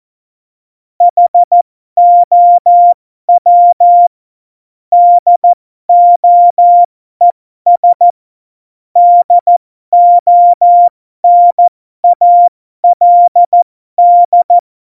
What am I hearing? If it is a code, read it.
Morse code, heard as a keyed tone, about 14 wpm, spaced as for 12 wpm, 700 Hz. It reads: HOW DOES DONALD
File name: Morsekurs / Witze